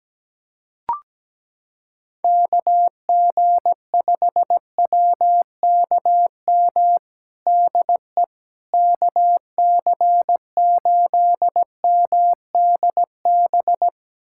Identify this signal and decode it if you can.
Morse code, heard as a keyed tone, about 17 wpm, 700 Hz. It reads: KG5WKM DE KC8MDB
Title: USA Callsign DE Callsign 17wpm